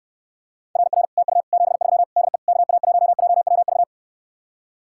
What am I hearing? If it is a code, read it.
Morse code, heard as a keyed tone, about 69 wpm, 700 Hz. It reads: RR TU 73 DE KA1JWV